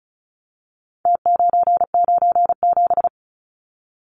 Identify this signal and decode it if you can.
Morse code, heard as a keyed tone, about 34 wpm, 700 Hz. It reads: T997
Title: Numbers - Contest Style - 4-Digit 35wpm